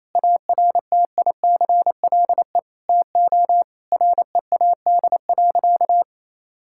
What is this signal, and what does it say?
Morse code, heard as a keyed tone, about 28 wpm, 700 Hz. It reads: ARTICLE TO READ.
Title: Sentences from Top 1000 Words - Review Part 5 - 28wpm